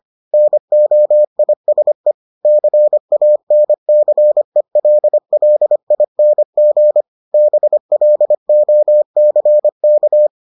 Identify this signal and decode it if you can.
Morse code, heard as a keyed tone, about 25 wpm, 600 Hz. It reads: NOISE CANCELLING BLOCK